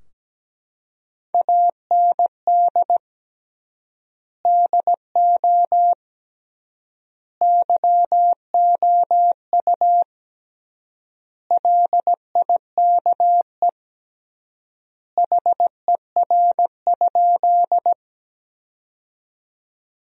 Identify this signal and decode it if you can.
Morse code, heard as a keyed tone, about 17 wpm, 700 Hz. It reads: AND DO YOU LIKE HER?
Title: 3x Word-Spacing - Sentences from Top 100 Words - 17wpm